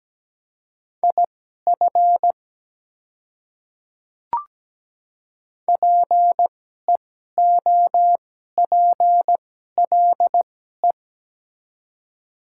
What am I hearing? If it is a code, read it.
Morse code, heard as a keyed tone, about 17 wpm, spaced as for 12 wpm, 690 Hz. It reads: IF PEOPLE
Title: 2x Character Spacing - Top 100 Words - 17wpm